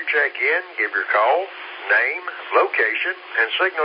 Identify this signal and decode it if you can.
Emergency Alert System (SAME) tones, heard as harsh buzzing bursts, with speech. It is heard from 0.7 to 3.5 s.